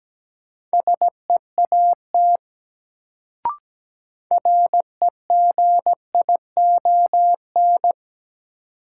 Morse code, heard as a keyed tone, about 17 wpm, 700 Hz. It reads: SEAT REGION
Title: Sentences from Top 1500 Words - New Word Drill Alt - 17wpm